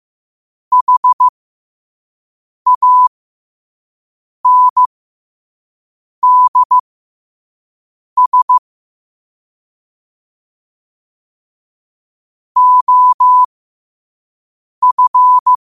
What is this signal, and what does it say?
Morse code, heard as a keyed tone, about 15 wpm, spaced as for 5 wpm, 1000 Hz. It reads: HANDS OF
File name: qotdfiles / 5wpm